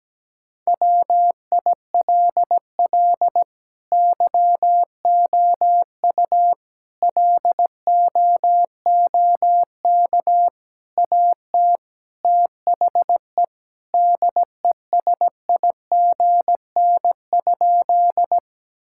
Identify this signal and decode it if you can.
Morse code, heard as a keyed tone, about 17 wpm, 700 Hz. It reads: WILL YOU LOOK AT THE DESIGN?
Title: Sentences from Top 1000 Words - Review Part 5 - 17wpm